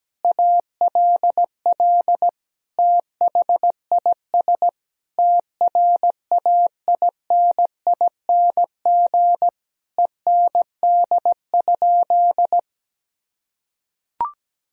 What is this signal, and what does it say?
Morse code, heard as a keyed tone, about 17 wpm, 700 Hz. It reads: ALL THIS TRAINING END?
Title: Sentences from Top 1000 Words - Review Part 4 - 17wpm